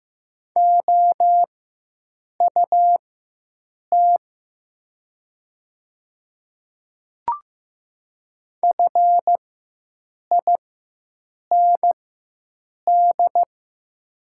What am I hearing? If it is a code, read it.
Morse code, heard as a keyed tone, about 15 wpm, spaced as for 7 wpm, 700 Hz. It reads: OUT FIND
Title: 4x Character Spacing - Top 100 Words - 15wpm